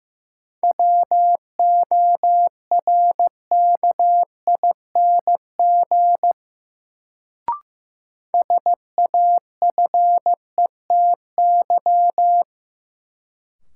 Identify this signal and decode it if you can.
Morse code, heard as a keyed tone, about 15 wpm, 700 Hz. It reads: WORKING SAFETY
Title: Sentences from Top 700 Words - New Word Drill - 15wpm